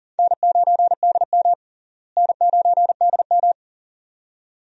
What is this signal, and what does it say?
Morse code, heard as a keyed tone, about 40 wpm, 700 Hz. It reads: N9DM N9DM
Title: USA Call Signs - 4 Characters - Repeated 2x - 40wpm